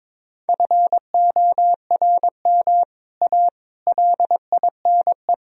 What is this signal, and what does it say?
Morse code, heard as a keyed tone, about 22 wpm, 700 Hz. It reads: FORM A LINE
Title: Sentences from Top 1000 Words - Review Part 1 - 22wpm